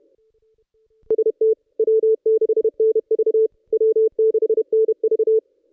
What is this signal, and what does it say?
Morse code, heard as a keyed tone, about 30 wpm, 440 Hz. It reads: ST W6NV W6NV
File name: contests / ss-cw-k5tr-op-n5rz -1800